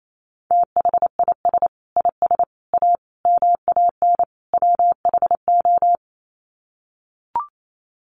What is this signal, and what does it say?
Morse code, heard as a keyed tone, about 28 wpm, 700 Hz. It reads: THIS IS A MAN WHO
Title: Sets of 5 Words Encore2 28wpm